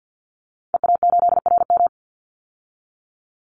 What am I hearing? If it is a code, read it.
Morse code, heard as a keyed tone, about 50 wpm, 700 Hz. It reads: EA8RM
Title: Call Signs - 50wpm